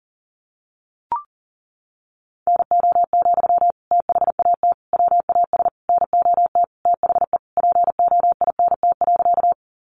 Morse code, heard as a keyed tone, about 40 wpm, 700 Hz. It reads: NO, THAT WAS NOT THE POINT.
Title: Sentences from Top 600 Words 40wpm